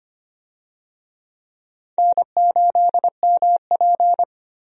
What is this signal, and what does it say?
Morse code, heard as a keyed tone, about 25 wpm, 700 Hz. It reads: N8MP
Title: USA Call Signs - 4 Characters - 25wpm